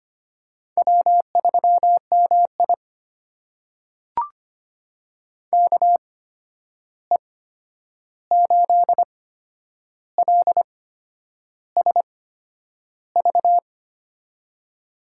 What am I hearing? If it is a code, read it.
Morse code, heard as a keyed tone, about 25 wpm, spaced as for 9 wpm, 700 Hz. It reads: W3MI KE8LSV
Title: USA Call Signs - 8x Character Spacing - 25wpm